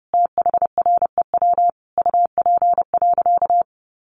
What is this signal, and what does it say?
Morse code, heard as a keyed tone, about 30 wpm, 700 Hz. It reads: THREW UP.
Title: A Sound Of Thunder 30wpm